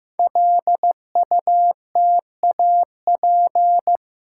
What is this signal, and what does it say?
Morse code, heard as a keyed tone, about 15 wpm, 700 Hz. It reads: LUTAP
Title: Code Groups - 5 Letters - 15wpm